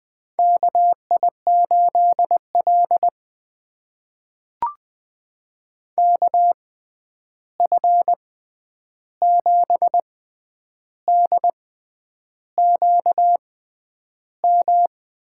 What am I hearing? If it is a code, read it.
Morse code, heard as a keyed tone, about 20 wpm, spaced as for 9 wpm, 700 Hz. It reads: KI8L KF7DQM
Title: USA Call Signs - 6x Character Spacing - 20wpm